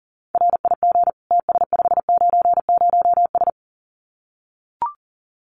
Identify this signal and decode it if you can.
Morse code, heard as a keyed tone, about 40 wpm, 700 Hz. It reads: RIG TS590S
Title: Common QSO Phrases Encore 40wpm